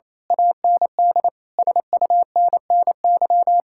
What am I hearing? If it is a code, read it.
Morse code, heard as a keyed tone, about 27 wpm, 700 Hz. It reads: AND SUNNY
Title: Speed-Racing Common QSO Phrases - 28wpm